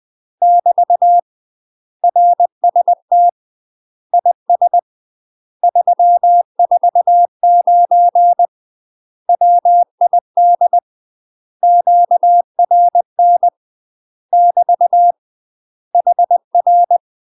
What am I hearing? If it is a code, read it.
Morse code, heard as a keyed tone, about 20 wpm, 700 Hz. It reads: <BT> RST IS 349 WID QRN <BT> HR
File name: UK MASTER QSOs TRAINING / 20WPM / UK MASTER QSOs 0040